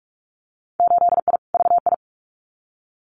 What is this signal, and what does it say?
Morse code, heard as a keyed tone, about 44 wpm, 700 Hz. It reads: 8I VI
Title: Winter Field Day Exchanges Encore 45wpm